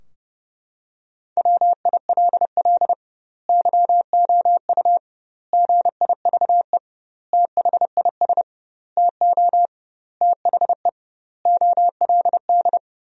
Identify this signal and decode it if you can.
Morse code, heard as a keyed tone, about 32 wpm, 700 Hz. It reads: WILL YOU GIVE THIS TO THE OLD
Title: 2x Word-Spacing - Sentences from Top 200 Words - 30wpm